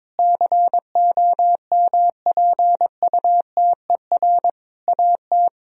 Morse code, heard as a keyed tone, about 22 wpm, 700 Hz. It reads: COMPUTER AT
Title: Sentences from Top 1000 Words - Review Part 3 - 22wpm